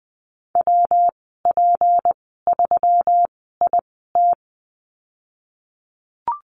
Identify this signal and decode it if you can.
Morse code, heard as a keyed tone, about 20 wpm, spaced as for 15 wpm, 700 Hz. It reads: WP3IT
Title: 167 Lesson 27g - Number 7 - Call Signs 20wpm